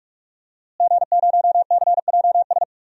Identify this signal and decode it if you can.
Morse code, heard as a keyed tone, about 42 wpm, 700 Hz. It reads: G0CJS
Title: DXCC-100 Call Signs - 5 Characters - Repeated 2x - 45wpm